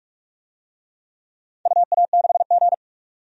Morse code, heard as a keyed tone, about 42 wpm, 700 Hz. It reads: UA6G
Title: DXCC-100 Call Signs - 4 Characters - 45wpm